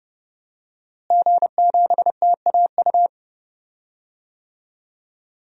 Morse code, heard as a keyed tone, about 31 wpm, 700 Hz. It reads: G7TAU
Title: DXCC-100 Call Signs - 5 Characters - Repeated 2x - 30wpm